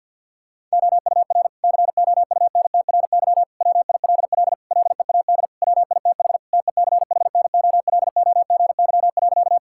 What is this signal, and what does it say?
Morse code, heard as a keyed tone, about 50 wpm, 700 Hz. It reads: OUR COUNTRY WILL LEAD WITH TECHNOLOGY.
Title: Sentences from Top 1000 Words - Review Part 5 - 50wpm